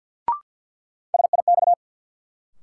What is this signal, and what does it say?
Morse code, heard as a keyed tone, about 50 wpm, 700 Hz. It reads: SIX
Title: Sentences from Top 600 Words - New Word Drill - 50wpm